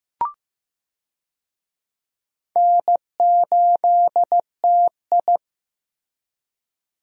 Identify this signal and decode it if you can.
Morse code, heard as a keyed tone, about 15 wpm, 700 Hz. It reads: N8TI
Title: USA Call Signs - 4 Characters - Repeated 2x - 15wpm